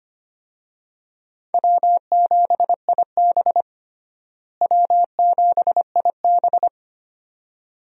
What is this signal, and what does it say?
Morse code, heard as a keyed tone, about 25 wpm, 700 Hz. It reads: W7IB W7IB
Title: USA Call Signs - 4 Characters - Repeated 2x - 25wpm